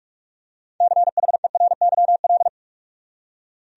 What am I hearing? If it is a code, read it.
Morse code, heard as a keyed tone, about 46 wpm, 700 Hz. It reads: CHERYL
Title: USA Common Names - 45wpm